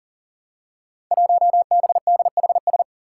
Morse code, heard as a keyed tone, about 42 wpm, 700 Hz. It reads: 1BDHS
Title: Five Challenging Letter-Number Combinations 40wpm